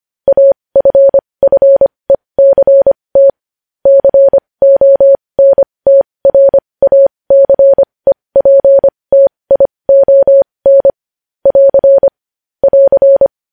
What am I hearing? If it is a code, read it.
Morse code, heard as a keyed tone, about 25 wpm, spaced as for 20 wpm, 550 Hz. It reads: AFFECT CONTRACEPTION <AR> <AR>
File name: podcast / News headlines-1x-25x20